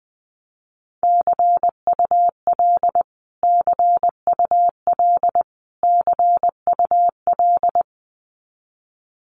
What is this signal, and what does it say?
Morse code, heard as a keyed tone, about 20 wpm, 700 Hz. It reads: CUL CUL CUL
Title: Abbreviations Repeated 3x - 20wpm